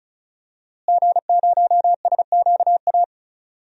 Morse code, heard as a keyed tone, about 34 wpm, 700 Hz. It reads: G0SQA
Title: DXCC-100 Call Signs - 5 Characters - Repeated 2x - 35wpm